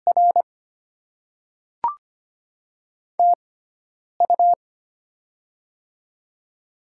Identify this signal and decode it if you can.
Morse code, heard as a keyed tone, about 25 wpm, spaced as for 9 wpm, 700 Hz. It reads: R TU
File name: Top 20 QSO Elements - 6x Character Spacing - 25wpm